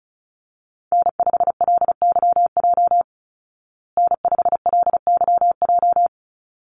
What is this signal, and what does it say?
Morse code, heard as a keyed tone, about 34 wpm, 700 Hz. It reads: N5LYJ N5LYJ
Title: Call Signs Repeated 3x - 35wpm